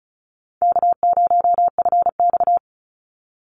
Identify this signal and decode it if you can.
Morse code, heard as a keyed tone, about 34 wpm, 700 Hz. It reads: K0FX
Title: Call Signs - 35wpm